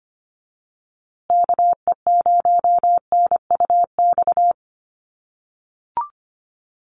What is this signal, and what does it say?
Morse code, heard as a keyed tone, about 25 wpm, 700 Hz. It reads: KE0NUX
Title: USA Call Signs Repeated 3x 25wpm